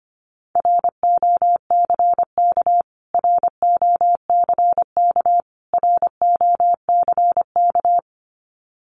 Morse code, heard as a keyed tone, about 25 wpm, 700 Hz. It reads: ROCK ROCK ROCK
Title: Top 500 Words Repeated 25wpm